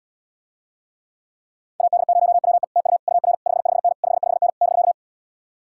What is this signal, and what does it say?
Morse code, heard as a keyed tone, about 77 wpm, 700 Hz. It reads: AD0WE TU UR 55N 55N <BK>